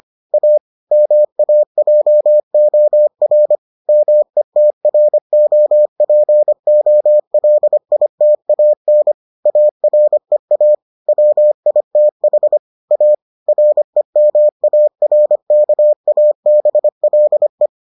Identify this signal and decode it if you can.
Morse code, heard as a keyed tone, about 25 wpm, 600 Hz. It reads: A MAJOR METROPOLITAN AREA WITH A REMARKABLE